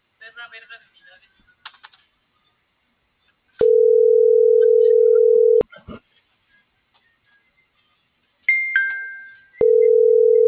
Ringback tone (North America), 440 + 480 Hz.
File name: CENTRAL MABE / out cristopher.carrera